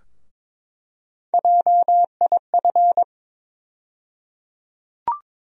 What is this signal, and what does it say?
Morse code, heard as a keyed tone, about 21 wpm, 700 Hz. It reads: JIF
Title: USA Call Sign Postfixes - 3 Letters 22wpm